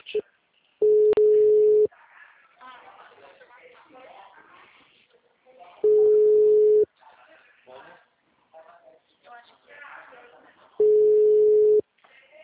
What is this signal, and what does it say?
Ringback tone (Europe, 425 Hz), with speech.
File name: out- -2047- - - .38441